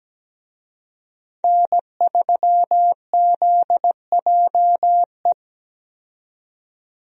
Morse code, heard as a keyed tone, about 17 wpm, 700 Hz. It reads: N3ZJE
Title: USA Call Signs - 5 Characters - 17wpm